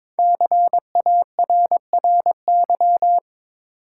Morse code, heard as a keyed tone, about 22 wpm, 700 Hz. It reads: CARRY
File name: 5 Letter Words - First 3 Letters Given - 22wpm